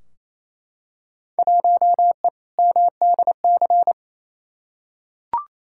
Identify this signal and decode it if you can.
Morse code, heard as a keyed tone, about 28 wpm, 700 Hz. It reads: 1E MDC
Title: Field Day Exchanges Encore - 28wpm